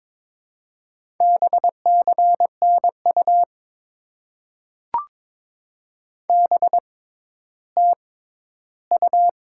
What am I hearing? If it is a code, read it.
Morse code, heard as a keyed tone, about 22 wpm, spaced as for 13 wpm, 700 Hz. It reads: BCNU BTU
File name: Top 90 QSO Elements - 6x Character Spacing - 22wpm